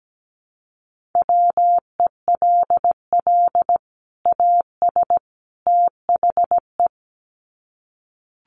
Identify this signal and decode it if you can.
Morse code, heard as a keyed tone, about 17 wpm, 700 Hz. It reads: WELL AS THE
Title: Sets of 3 Words 17wpm